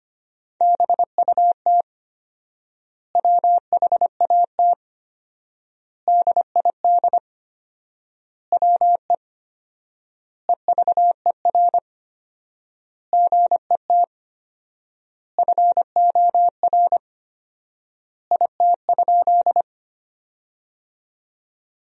Morse code, heard as a keyed tone, about 25 wpm, 700 Hz. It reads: BUT WHAT DID WE EVER GET FOR IT?
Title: 4x Word-Spacing - Sentences from Top 400 Words - 25wpm